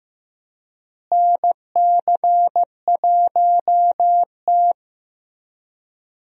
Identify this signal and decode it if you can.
Morse code, heard as a keyed tone, about 15 wpm, 700 Hz. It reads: NC1T
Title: USA Call Signs - 4 Characters - Repeated 3x - 15wpm